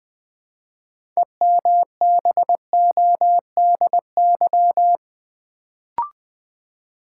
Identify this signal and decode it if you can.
Morse code, heard as a keyed tone, about 20 wpm, 700 Hz. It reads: EMBODY
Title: Build A Word - 3 Letters At A Time - 6-Letter Words - 20wpm